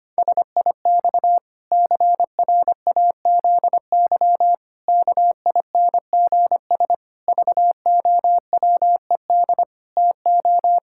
Morse code, heard as a keyed tone, about 25 wpm, 700 Hz. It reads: SIX CRAZY KINGS VOWED TO
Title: Pangram Challenge v2 - 25wpm